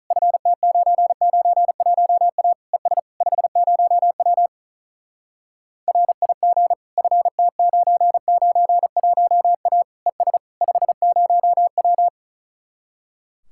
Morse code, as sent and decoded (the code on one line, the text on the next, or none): ..-. - ----. ----. .---- .- / . ... / ..... ----- .-- / .-. .. --. / ..-. - ----. ----. .---- .- / . ... / ..... ----- .--
FT991A ES 50W RIG FT991A ES 50W